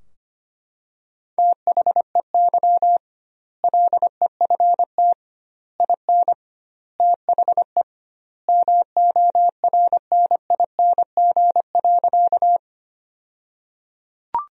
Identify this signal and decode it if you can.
Morse code, heard as a keyed tone, about 25 wpm, 700 Hz. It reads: THEY LEFT IN THE MORNING.
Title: 2x Word-Spacing - Sentences from Top 400 Words - 25wpm